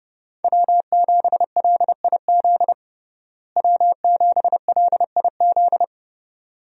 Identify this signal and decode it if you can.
Morse code, heard as a keyed tone, about 30 wpm, 700 Hz. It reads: W7LIZ W7LIZ
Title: USA Call Signs - 5 Characters - Repeated 2x - 30wpm